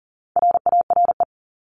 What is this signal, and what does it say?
Morse code, heard as a keyed tone, about 40 wpm, 700 Hz. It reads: RARE
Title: Top 1001-1250 Words 40wpm